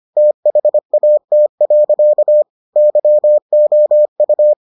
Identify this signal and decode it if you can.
Morse code, heard as a keyed tone, about 25 wpm, 600 Hz. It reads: THAT. YOU